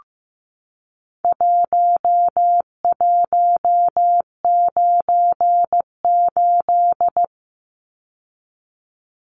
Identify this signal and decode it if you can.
Morse code, heard as a keyed tone, about 15 wpm, 700 Hz. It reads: 1198
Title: Numbers - Contest Style - 4-Digit 15wpm